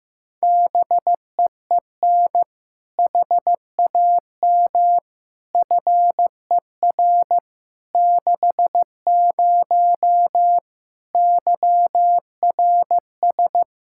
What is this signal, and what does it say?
Morse code, heard as a keyed tone, about 15 wpm, 700 Hz. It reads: BEEN HAM FER 60 YRS
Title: K1USN SST QSO Practice 15wpm